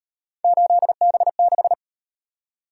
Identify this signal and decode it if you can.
Morse code, heard as a keyed tone, about 37 wpm, 700 Hz. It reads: 8B6